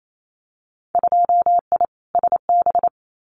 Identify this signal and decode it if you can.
Morse code, heard as a keyed tone, about 28 wpm, 700 Hz. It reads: 2I SB
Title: Winter Field Day Exchanges Encore 28wpm